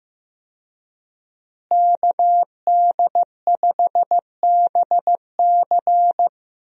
Morse code, heard as a keyed tone, about 15 wpm, 700 Hz. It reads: KD5BC
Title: USA Call Signs - 5 Characters - Repeated 2x - 15wpm